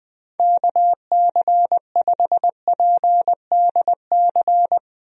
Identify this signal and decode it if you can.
Morse code, heard as a keyed tone, about 20 wpm, 700 Hz. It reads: KC5PDC
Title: USA Call Signs - 6 Characters - 20wpm